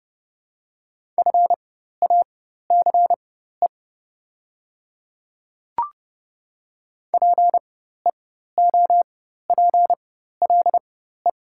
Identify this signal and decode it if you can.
Morse code, heard as a keyed tone, about 30 wpm, spaced as for 14 wpm, 690 Hz. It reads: FACE PEOPLE
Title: 4x Character Spacing - Top 300 Words - 30wpm